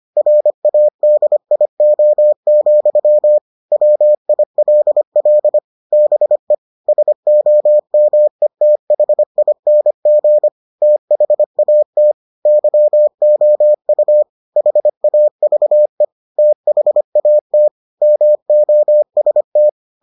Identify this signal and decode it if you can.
Morse code, heard as a keyed tone, about 25 wpm, 600 Hz. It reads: RADIO, WILL BE SOMETHING THAT YOU HAVE THAT MOST